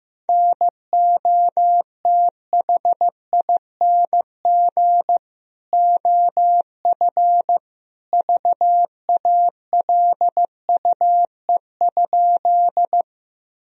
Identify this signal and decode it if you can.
Morse code, heard as a keyed tone, about 16 wpm, 700 Hz. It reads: NOTHING OF VALUE?
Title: Sentences from Top 1000 Words - Review Part 3 - 15wpm